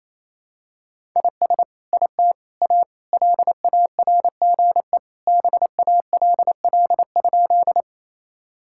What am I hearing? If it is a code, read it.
Morse code, heard as a keyed tone, about 28 wpm, 700 Hz. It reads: IS IT A LARGE BALL?
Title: Sentences from Top 1000 Words - Review Part 6 - 28wpm